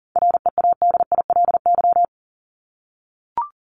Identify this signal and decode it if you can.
Morse code, heard as a keyed tone, about 40 wpm, 700 Hz. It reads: READILY
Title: Top 1751-2000 Words 40wpm